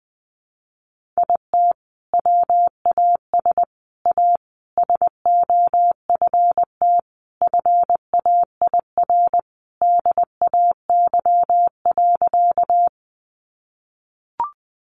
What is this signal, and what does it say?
Morse code, heard as a keyed tone, about 20 wpm, 700 Hz. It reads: IT WAS A SOFT FAIR DAY.
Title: Sentences from Top 1500 Words 20wpm